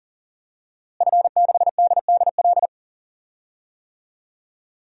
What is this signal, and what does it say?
Morse code, heard as a keyed tone, about 40 wpm, 700 Hz. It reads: F6DDL
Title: DXCC-100 Call Signs - 5 Characters - Repeated 3x - 40wpm